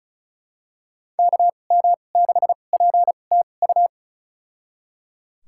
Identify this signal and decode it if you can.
Morse code, heard as a keyed tone, about 36 wpm, spaced as for 25 wpm, 700 Hz. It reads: KM6PTU
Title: USA Call Signs - 2x Character Spacing - 35wpm